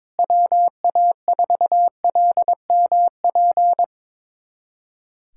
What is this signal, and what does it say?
Morse code, heard as a keyed tone, about 22 wpm, 700 Hz. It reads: WA4LMP
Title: USA Callsign DE Callsign - 6 Characters - 22wpm